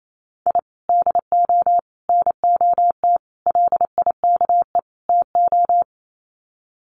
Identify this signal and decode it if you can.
Morse code, heard as a keyed tone, about 28 wpm, 700 Hz. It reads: I DO NOT LIKE TO
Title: Sets of 5 Words 28wpm